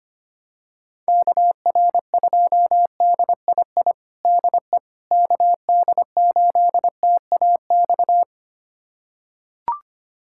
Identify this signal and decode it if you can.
Morse code, heard as a keyed tone, about 25 wpm, 700 Hz. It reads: KR2DII DE KD8TAX
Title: USA Callsign DE Callsign 25wpm